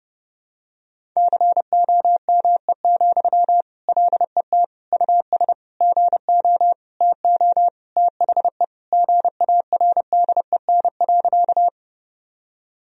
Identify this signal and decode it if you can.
Morse code, heard as a keyed tone, about 30 wpm, 700 Hz. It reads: COME, LET US GO TO THE GARDEN.
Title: Sentences from Top 1000 Words - Review Part 4 - 30wpm